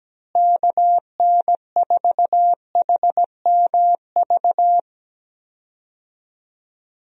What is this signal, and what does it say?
Morse code, heard as a keyed tone, about 17 wpm, 700 Hz. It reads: KN4HMV
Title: USA Call Signs - 6 Characters - Repeated 3x - 17wpm